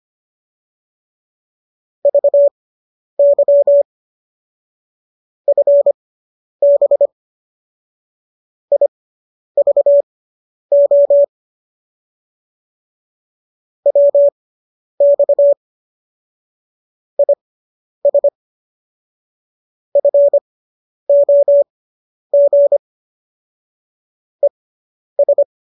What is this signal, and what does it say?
Morse code, heard as a keyed tone, about 25 wpm, spaced as for 10 wpm, 570 Hz. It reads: VY FB IVO WX IS FOG ES
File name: qso219 10